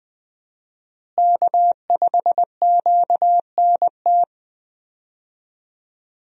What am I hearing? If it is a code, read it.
Morse code, heard as a keyed tone, about 20 wpm, 700 Hz. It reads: K5QNT
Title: USA Call Signs - 5 Characters - Repeated 2x - 20wpm